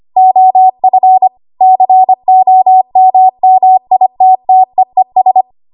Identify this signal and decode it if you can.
Morse code, heard as a keyed tone, about 25 wpm, 750 Hz. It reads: OF COMMITTEES
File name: Morse / 25WPM